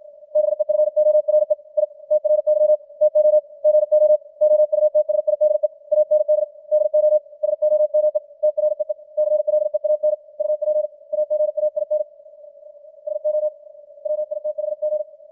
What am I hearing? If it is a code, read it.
Morse code, heard as a keyed tone, about 56 wpm, 610 Hz. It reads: BEFORE I TRY TO GO OUTSIDE AND DO SOME TREE CLEAN UP AGAIN SO WITHG
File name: archived / CWBros